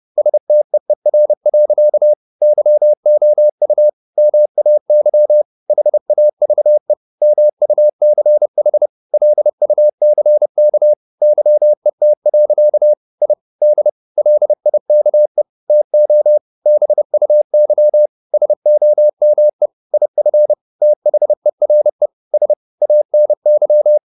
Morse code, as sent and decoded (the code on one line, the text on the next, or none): ... - . .-. .-.-.- / -.-- --- ..- / -- .- -.-- / .... .- ...- . / -- ..- -.-. .... / .-.. ..- -.-. -.- / -.-- . - .-.-.- / .. / -.. / .-.. .. -.- . / - --- / -... ..- -.-- / ... --- -- . / .. ..-. / - .... .-. / ... / .- -. -.--
STEER. YOU MAY HAVE MUCH LUCK YET. I D LIKE TO BUY SOME IF THERE S ANY